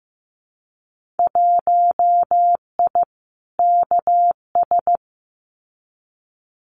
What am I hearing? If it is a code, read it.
Morse code, heard as a keyed tone, about 15 wpm, 700 Hz. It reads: 1I KS
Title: Winter Field Day Exchanges Encore 15wpm